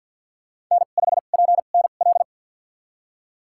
Morse code, heard as a keyed tone, about 51 wpm, spaced as for 36 wpm, 700 Hz. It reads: N5PNL
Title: USA Call Signs - 5 Characters - 2x Character Spacing - 50wpm